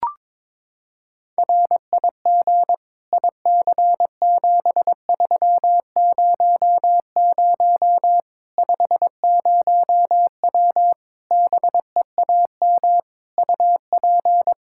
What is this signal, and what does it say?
Morse code, heard as a keyed tone, about 22 wpm, 700 Hz. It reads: RIG IC7300 50W BEAM UP
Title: Practice QSO - Exchange 2 - Info - 22wpm